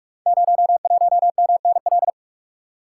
Morse code, heard as a keyed tone, about 44 wpm, 700 Hz. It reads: 01MNL
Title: Code Groups - 5 Letter:Numbers - 45wpm